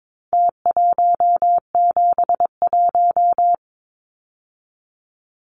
Morse code, heard as a keyed tone, about 22 wpm, 700 Hz. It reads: T171
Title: Numbers - Contest Style - 4-Digit 22wpm